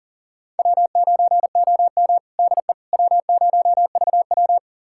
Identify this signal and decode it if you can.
Morse code, heard as a keyed tone, about 40 wpm, 700 Hz. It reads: W9OM DE W0VW
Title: USA Callsign DE Callsign - 4 Characters - 40wpm